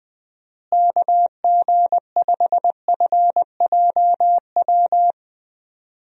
Morse code, heard as a keyed tone, about 20 wpm, 700 Hz. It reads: KG5FJW
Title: USA Call Signs - 6 Characters - Repeated 2x - 20wpm